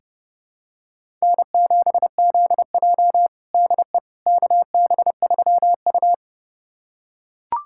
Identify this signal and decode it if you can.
Morse code, heard as a keyed tone, about 30 wpm, 700 Hz. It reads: N7ZJ DE KB3U
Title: USA Callsign DE Callsign - 4 Characters - 30wpm